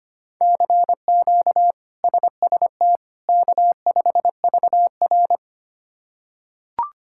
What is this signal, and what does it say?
Morse code, heard as a keyed tone, about 25 wpm, 700 Hz. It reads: CQ SST K5VR
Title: K1USN SST QSO Practice - 25wpm